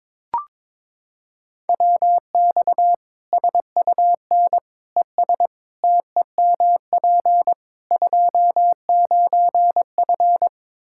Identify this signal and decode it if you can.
Morse code, heard as a keyed tone, about 22 wpm, 700 Hz. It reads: WX SUN ES TEMP 29F
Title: K1USN SST QSO Practice 22wpm